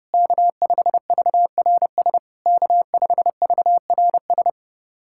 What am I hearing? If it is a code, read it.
Morse code, heard as a keyed tone, about 29 wpm, 690 Hz. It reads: K5VRS K5VRS
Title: SOTA QSO Practice 30wpm